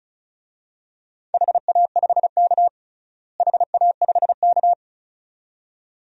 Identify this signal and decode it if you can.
Morse code, heard as a keyed tone, about 36 wpm, 700 Hz. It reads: HA5K HA5K
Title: DXCC-100 Call Signs - 4 Characters - Repeated 2x - 35wpm